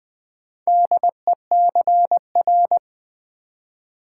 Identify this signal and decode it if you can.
Morse code, heard as a keyed tone, about 20 wpm, 700 Hz. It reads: DECR
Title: Build A Word - 2 Letters At A Time - 8-Letter Words - 20wpm